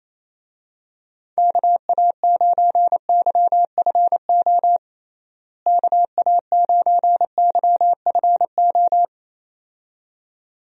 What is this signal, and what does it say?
Morse code, heard as a keyed tone, about 28 wpm, 700 Hz. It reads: KA9YFO KA9YFO
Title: USA Call Signs - 6 Characters - Repeated 2x - 28wpm